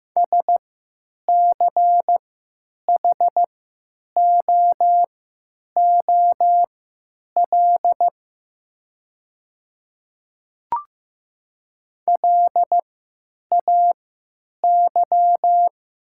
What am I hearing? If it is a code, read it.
Morse code, heard as a keyed tone, about 15 wpm, spaced as for 9 wpm, 700 Hz. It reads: SCHOOL LAY
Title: 3x Character Spacing - Top 500 Words - 15wpm